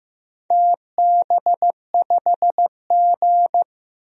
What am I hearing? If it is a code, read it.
Morse code, heard as a keyed tone, about 16 wpm, 700 Hz. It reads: TB5G
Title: DXCC-100 Call Signs - 4 Characters - Repeated 2x - 15wpm